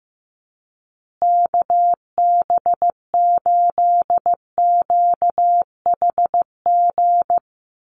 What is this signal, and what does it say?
Morse code, heard as a keyed tone, about 14 wpm, 700 Hz. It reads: KB8QHG
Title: USA Call Signs Repeated 2x 15wpm